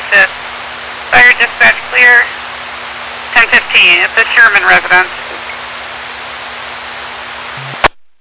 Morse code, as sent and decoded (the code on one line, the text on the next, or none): none